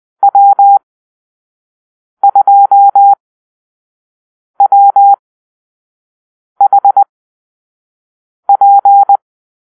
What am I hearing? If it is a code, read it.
Morse code, heard as a keyed tone, about 20 wpm, spaced as for 5 wpm, 800 Hz. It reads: W2WHP
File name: arrl / Call-Sign Drill-1